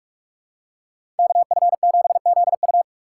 Morse code, heard as a keyed tone, about 46 wpm, 700 Hz. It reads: KF7ZU
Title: USA Call Signs - 4x Character Spacing - 45wpm